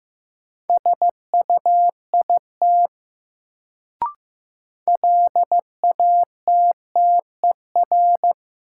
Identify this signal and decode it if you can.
Morse code, heard as a keyed tone, about 15 wpm, 700 Hz. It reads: SUIT LATTER
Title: Sentences from Top 1600 Words - New Word Drill Alt - 15wpm